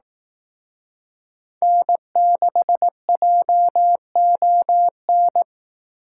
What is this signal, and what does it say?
Morse code, heard as a keyed tone, about 18 wpm, 700 Hz. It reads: N6JON
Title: Speed Racing - USA Call Signs 15wpm